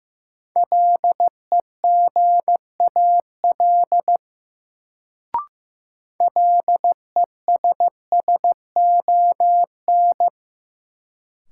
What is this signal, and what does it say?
Morse code, heard as a keyed tone, about 16 wpm, 700 Hz. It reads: LEGAL LESSON
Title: Sentences from Top 1200 Words - New Word Drill - 15wpm